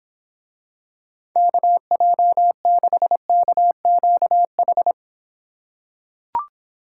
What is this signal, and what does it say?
Morse code, heard as a keyed tone, about 26 wpm, 700 Hz. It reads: KJ6KQH